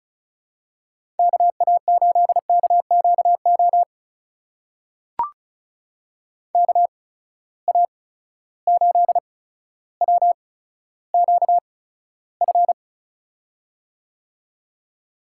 Morse code, heard as a keyed tone, about 36 wpm, spaced as for 15 wpm, 700 Hz. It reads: KA8KQO KA8WQF
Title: USA Call Signs - 6 Characters - 8x Character Spacing - 35wpm